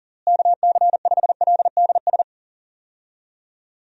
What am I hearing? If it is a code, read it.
Morse code, heard as a keyed tone, about 39 wpm, 700 Hz. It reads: KC5LDS
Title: USA Call Signs - 6 Characters - Repeated 3x - 40wpm